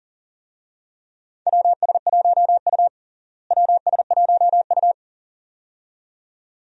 Morse code, heard as a keyed tone, about 40 wpm, 700 Hz. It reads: WS1U WS1U
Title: USA Call Signs - 4 Characters - Repeated 2x - 40wpm